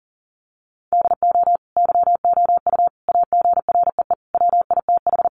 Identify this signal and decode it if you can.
Morse code, heard as a keyed tone, about 37 wpm, 700 Hz. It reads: DO YOU AGREE WITH
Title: Sets of 5 Words Encore 40wpm